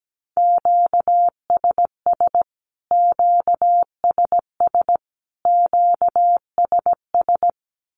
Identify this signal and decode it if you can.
Morse code, heard as a keyed tone, about 17 wpm, 700 Hz. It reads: QSS QSS QSS
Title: Q-Codes Repeated 3x 17wpm